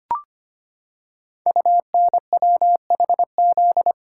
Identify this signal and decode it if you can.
Morse code, heard as a keyed tone, about 25 wpm, 700 Hz. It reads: UNWHZ
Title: Code Groups - 5 Letters - 25wpm